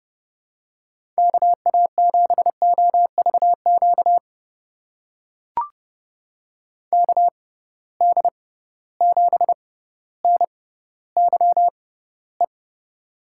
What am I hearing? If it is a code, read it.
Morse code, heard as a keyed tone, about 30 wpm, spaced as for 15 wpm, 700 Hz. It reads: KA7OVQ KD7NYE
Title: USA Call Signs - 6 Characters - 6x Character Spacing - 30wpm